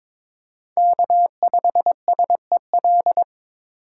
Morse code, heard as a keyed tone, about 22 wpm, 700 Hz. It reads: K5SEL
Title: USA Call Signs - 5 Characters - 8x Character Spacing - 22wpm